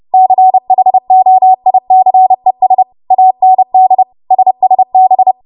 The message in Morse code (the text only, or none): CHOICES AND SSB